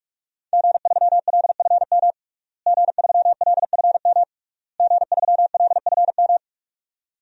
Morse code, heard as a keyed tone, about 46 wpm, 700 Hz. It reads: G3LFM G3LFM G3LFM
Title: DXCC-100 Call Signs - 5 Characters - Repeated 3x - 45wpm